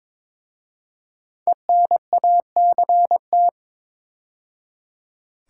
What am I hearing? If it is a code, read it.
Morse code, heard as a keyed tone, about 23 wpm, 700 Hz. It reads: ENACT
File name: 5 Letter Words - First 3 Letters Given - 22wpm